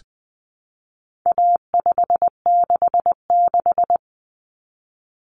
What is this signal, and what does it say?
Morse code, heard as a keyed tone, about 20 wpm, 700 Hz. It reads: A566
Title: Common Cut Numbers - 4-Digit 20wpm